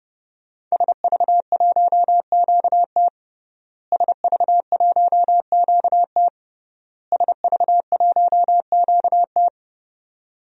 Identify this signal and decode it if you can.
Morse code, heard as a keyed tone, about 32 wpm, 700 Hz. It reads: SV1QT SV1QT SV1QT
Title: DXCC-100 Call Signs - 5 Characters - Repeated 3x - 30wpm